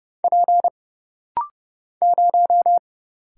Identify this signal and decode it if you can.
Morse code, heard as a keyed tone, about 30 wpm, 700 Hz. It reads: P 0
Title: Single Letter-Number Alt - Rapid-Fire 30wpm